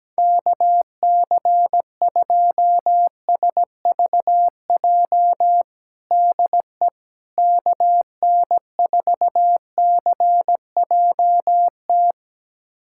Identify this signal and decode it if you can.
Morse code, heard as a keyed tone, about 17 wpm, 700 Hz. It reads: KC2SVJ DE KN4CJT
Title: USA Callsign DE Callsign - 6 Characters - 17wpm